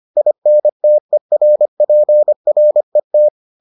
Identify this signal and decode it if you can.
Morse code, heard as a keyed tone, about 25 wpm, 600 Hz. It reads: INTERPRET